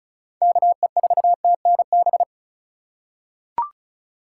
Morse code, heard as a keyed tone, about 36 wpm, 700 Hz. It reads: KE4TNB
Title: USA Call Signs - 4x Character Spacing - 35wpm